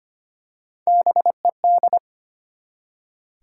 Morse code, heard as a keyed tone, about 25 wpm, 700 Hz. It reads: BED
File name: practice / 25wpm / Easy Words with Hints - 25wpm